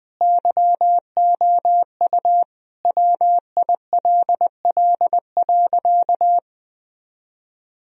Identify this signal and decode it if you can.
Morse code, heard as a keyed tone, about 20 wpm, 700 Hz. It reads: YOU WILL.
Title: Sentences from Top 1000 Words - Review Part 2 - 20wpm